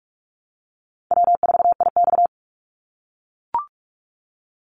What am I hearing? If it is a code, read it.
Morse code, heard as a keyed tone, about 44 wpm, 700 Hz. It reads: W4IX
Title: Easy Call Signs - 45wpm